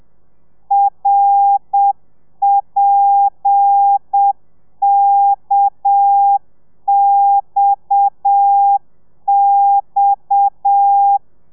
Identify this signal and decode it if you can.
Morse code, heard as a keyed tone, about 7 wpm, 800 Hz. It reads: RPKXX
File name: telegrafia / i2tfj / Lez7 30